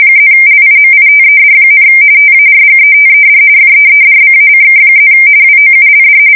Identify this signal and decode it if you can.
RTTY, about 45.45 baud. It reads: +LIMITED, TARGETED FEEDBACK+ WOULD
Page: 170 Shift 2125/2295